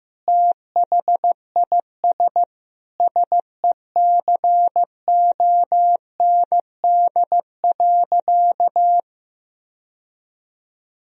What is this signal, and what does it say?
Morse code, heard as a keyed tone, about 15 wpm, 700 Hz. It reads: THIS SECOND.
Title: Sentences from Top 1000 Words - Review Part 1 - 15wpm